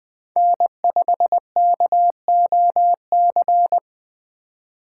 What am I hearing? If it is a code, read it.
Morse code, heard as a keyed tone, about 20 wpm, 700 Hz. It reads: N5KOC
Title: Speed Racing - USA 1x3 Call Signs 17wpm